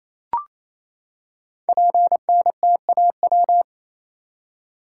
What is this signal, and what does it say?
Morse code, heard as a keyed tone, about 27 wpm, 700 Hz. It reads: PNTAW
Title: Code Groups - 5 Letters - 28wpm